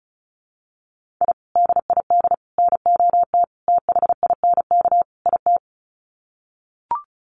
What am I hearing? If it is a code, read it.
Morse code, heard as a keyed tone, about 34 wpm, 700 Hz. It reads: I DID NOT THINK IT
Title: Sets of 5 Words Encore2 35wpm